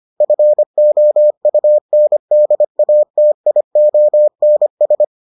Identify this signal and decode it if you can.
Morse code, heard as a keyed tone, about 25 wpm, 600 Hz. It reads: FOUNDATIONS